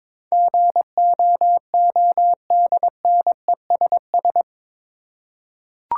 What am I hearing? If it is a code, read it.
Morse code, heard as a keyed tone, about 22 wpm, 700 Hz. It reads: GOODNESS
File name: practice / 22wpm / Eight Letter Words - 22wpm